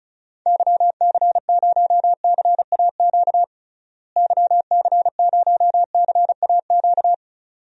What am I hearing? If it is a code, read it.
Morse code, heard as a keyed tone, about 34 wpm, 700 Hz. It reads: YC0CAQ YC0CAQ
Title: DXCC-100 Call Signs - 6 Characters - Repeated 2x - 35wpm